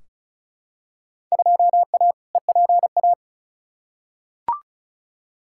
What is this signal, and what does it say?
Morse code, heard as a keyed tone, about 36 wpm, 700 Hz. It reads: 2A EPA
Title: Field Day Exchanges - 35wpm